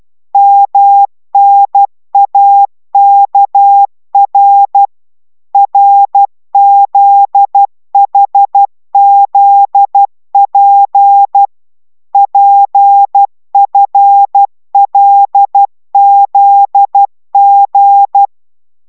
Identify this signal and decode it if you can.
Morse code, heard as a keyed tone, about 12 wpm, 800 Hz. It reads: MNAKR RZHZP PFLZG
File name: taviro tanulas / cw 69 15 10